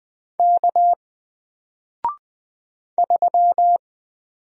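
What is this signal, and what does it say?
Morse code, heard as a keyed tone, about 20 wpm, 700 Hz. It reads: K 3
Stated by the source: Single letter and number Morse drill at 20 wpm, from N4LCD.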